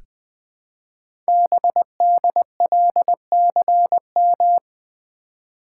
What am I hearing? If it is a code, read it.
Morse code, heard as a keyed tone, about 20 wpm, 700 Hz. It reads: BDLCM
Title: Code Groups - 5 Letters - 20wpm